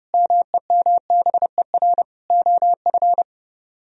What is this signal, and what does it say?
Morse code, heard as a keyed tone, about 33 wpm, 700 Hz. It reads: MEMBER OF